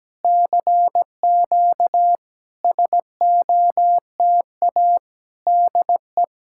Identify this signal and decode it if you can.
Morse code, heard as a keyed tone, about 17 wpm, 700 Hz. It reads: CQ SOTA DE
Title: SOTA QSO Practice 17wpm